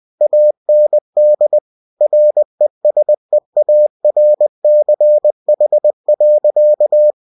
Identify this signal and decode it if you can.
Morse code, heard as a keyed tone, about 20 wpm, 600 Hz. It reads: AND RESEARCH.
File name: cw-books / winnie-the-pooh / winnie-20wpm-page-0112